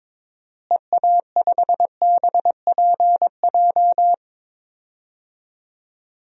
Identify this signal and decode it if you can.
Morse code, heard as a keyed tone, about 22 wpm, 700 Hz. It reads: EA5BPJ
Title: DXCC-100 Call Signs - 6 Characters - Repeated 3x - 22wpm